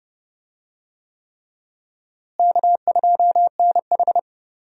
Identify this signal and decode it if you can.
Morse code, heard as a keyed tone, about 30 wpm, 700 Hz. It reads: K2NH
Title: USA Call Signs - 4 Characters - Repeated 3x - 30wpm